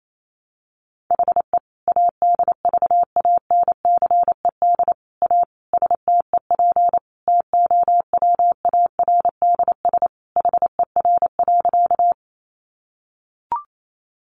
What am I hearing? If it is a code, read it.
Morse code, heard as a keyed tone, about 28 wpm, 700 Hz. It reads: HE ADVANCED A STEP TOWARDS HER.
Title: Sentences from Top 1200 Words 28wpm